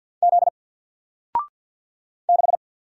Morse code, heard as a keyed tone, about 48 wpm, 700 Hz. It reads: Z 6
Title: Single Character (Letter:Number:Punctuation:Prosign) 50wpm